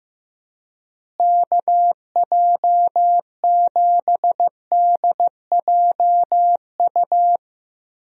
Morse code, heard as a keyed tone, about 15 wpm, 700 Hz. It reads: KJ7DJU
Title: USA Call Signs - 6 Characters - Repeated 3x - 15wpm